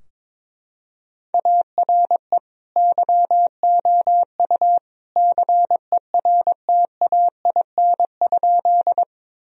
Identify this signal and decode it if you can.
Morse code, heard as a keyed tone, about 21 wpm, 700 Hz. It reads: ARE YOU CERTAIN?
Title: Sentences from Top 1000 Words - Review Part 2 - 22wpm